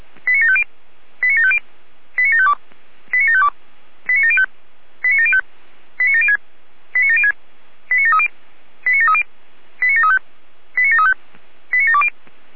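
Five-tone selective call (ZVEI), heard as quick runs of five tones, tones of about 70 ms. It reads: 89750 89750 89752 89752 89705 89705 89706 89706 89730 89730 89735 89735 89720
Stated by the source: ZVEI five-tone selective call received on SDR.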